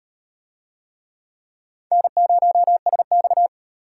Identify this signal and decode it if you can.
Morse code, heard as a keyed tone, about 38 wpm, 700 Hz. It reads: N0SX